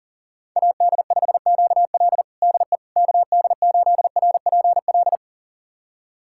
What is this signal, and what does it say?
Morse code, heard as a keyed tone, about 40 wpm, 700 Hz. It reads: AD5QL DE KD8RPL
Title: USA Callsign DE Callsign 40wpm